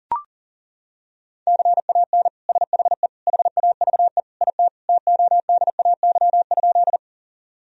Morse code, heard as a keyed tone, about 40 wpm, 700 Hz. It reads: CAN SHE HAVE IT TODAY?
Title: Sentences from Top 1000 Words - Review Part 4 - 40wpm